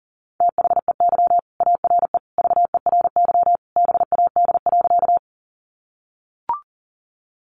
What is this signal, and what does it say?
Morse code, heard as a keyed tone, about 40 wpm, 700 Hz. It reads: THEY ARE VERY BAD.
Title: Sentences from Top 700 Words 40wpm